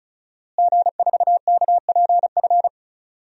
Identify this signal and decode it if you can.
Morse code, heard as a keyed tone, about 34 wpm, 700 Hz. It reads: G4KPF
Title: DXCC-100 Call Signs - 5 Characters - Repeated 3x - 35wpm